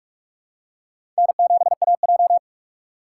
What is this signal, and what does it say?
Morse code, heard as a keyed tone, about 44 wpm, 700 Hz. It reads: N7AJ
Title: USA Call Signs - 4 Characters - 4x Character Spacing - 45wpm